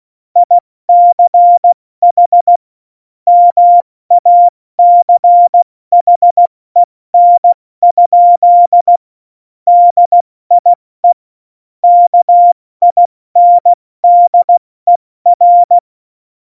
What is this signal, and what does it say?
Morse code, heard as a keyed tone, about 16 wpm, spaced as for 14 wpm, 700 Hz. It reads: ICH MACHEN? DIE KINDER
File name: Morsekurs / Witze / Witz 04. Februar 2020, 70 F80